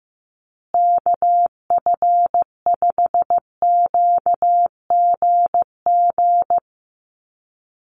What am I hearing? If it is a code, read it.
Morse code, heard as a keyed tone, about 15 wpm, 700 Hz. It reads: KF5QGG
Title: USA Call Signs Repeated 2x 15wpm